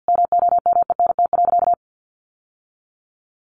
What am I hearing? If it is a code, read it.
Morse code, heard as a keyed tone, about 50 wpm, 700 Hz. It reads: MOMENT.
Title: Sentences from Top 1200 Words 50wpm